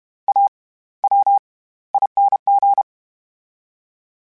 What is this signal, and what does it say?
Morse code, heard as a keyed tone, about 32 wpm, spaced as for 18 wpm, 800 Hz. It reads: AWING
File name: Special INGs / Special INGs-mp3s / SlowToFaster ing @30